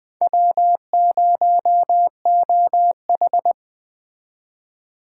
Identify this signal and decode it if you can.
Morse code, heard as a keyed tone, about 20 wpm, 700 Hz. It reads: W0OH
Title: Speed Racing - USA 1x2 Call Signs 17wpm